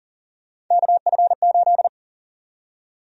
Morse code, heard as a keyed tone, about 39 wpm, 700 Hz. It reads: KF8
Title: USA Call Sign Prefixes - 40wpm